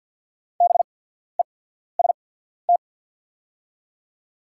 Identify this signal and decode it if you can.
Morse code, heard as a keyed tone, about 48 wpm, spaced as for 14 wpm, 700 Hz. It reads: BEST
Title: 8x Character Spacing - Top 500 Words - 50wpm